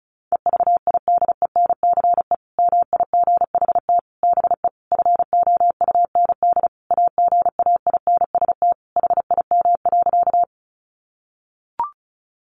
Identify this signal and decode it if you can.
Morse code, heard as a keyed tone, about 34 wpm, 700 Hz. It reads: EVIDENCE MIGHT BE FOUND AGAINST HIM.
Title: Sentences from Top 1300 Words 35wpm